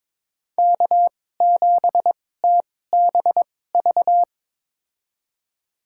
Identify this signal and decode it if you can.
Morse code, heard as a keyed tone, about 22 wpm, spaced as for 16 wpm, 700 Hz. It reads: K7TBV
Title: USA Call Signs - 5 Characters - 2x Character Spacing - 22wpm